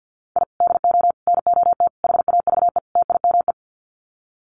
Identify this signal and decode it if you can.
Morse code, heard as a keyed tone, about 50 wpm, 700 Hz. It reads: I DO NOT HAVE TIME
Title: Sets of 5 Words Encore 50wpm